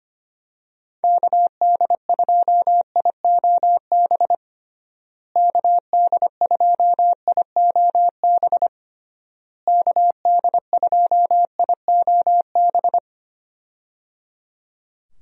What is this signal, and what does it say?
Morse code, heard as a keyed tone, about 25 wpm, 700 Hz. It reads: KD2IOB KD2IOB KD2IOB
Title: USA Call Signs - 6 Characters - Repeated 3x - 25wpm